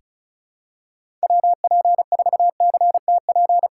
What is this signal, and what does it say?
Morse code, heard as a keyed tone, about 36 wpm, 700 Hz. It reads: WP4CTP
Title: USA Call Signs - 6 Characters - 2x Character Spacing - 35wpm